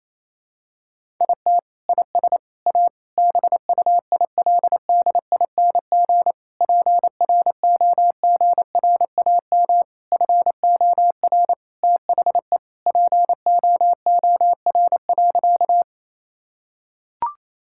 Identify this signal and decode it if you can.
Morse code, heard as a keyed tone, about 28 wpm, 700 Hz. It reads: IT IS A BUILDING PROGRAM FOR THE POOR.
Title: Sentences from Top 1000 Words - Review Part 5 - 28wpm